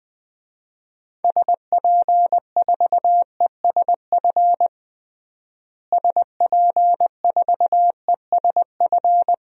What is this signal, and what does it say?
Morse code, heard as a keyed tone, about 20 wpm, 700 Hz. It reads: SP4ESF SP4ESF
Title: DXCC-100 Call Signs - 6 Characters - Repeated 2x - 20wpm